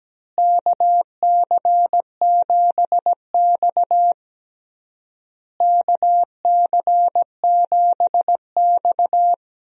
Morse code, heard as a keyed tone, about 17 wpm, 700 Hz. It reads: KC7X KC7X
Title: USA Call Signs - 4 Characters - Repeated 3x - 17wpm